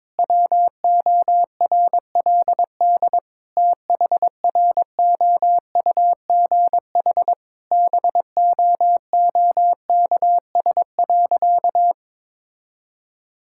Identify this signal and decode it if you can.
Morse code, heard as a keyed tone, about 22 wpm, 700 Hz. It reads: WORLD THROUGH BOOKS.
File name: Difficult Sentences with Hints - 22wpm